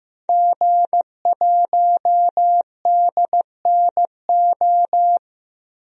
Morse code, heard as a keyed tone, about 15 wpm, 700 Hz. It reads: G1DNO
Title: DXCC-100 Call Signs - 5 Characters - 15wpm